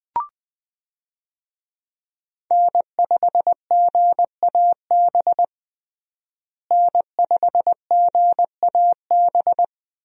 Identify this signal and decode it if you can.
Morse code, heard as a keyed tone, about 20 wpm, 700 Hz. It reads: N5GAB N5GAB
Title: USA Call Signs - 5 Characters - Repeated 2x - 20wpm